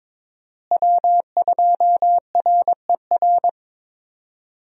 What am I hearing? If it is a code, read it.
Morse code, heard as a keyed tone, about 22 wpm, 700 Hz. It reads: W2RER
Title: USA Call Signs - 5 Characters - Repeated 2x - 22wpm